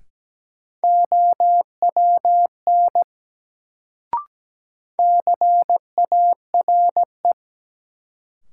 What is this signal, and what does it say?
Morse code, heard as a keyed tone, about 17 wpm, 700 Hz. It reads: OWN CARE
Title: Sentences from Top 300 Words - New Word Drill - 17wpm